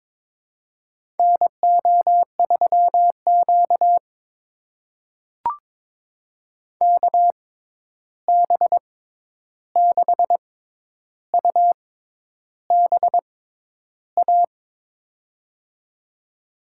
Morse code, heard as a keyed tone, about 22 wpm, spaced as for 10 wpm, 700 Hz. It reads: NO3Q KB6UBA
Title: USA Call Signs - 6x Character Spacing - 22wpm